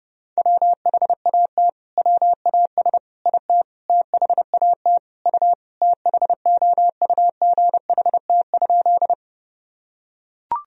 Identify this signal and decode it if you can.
Morse code, heard as a keyed tone, about 30 wpm, 700 Hz. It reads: WHAT WAS IT THAT U THOUGHT?
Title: Sentences from Top 300 Words and Abbreviations 30wpm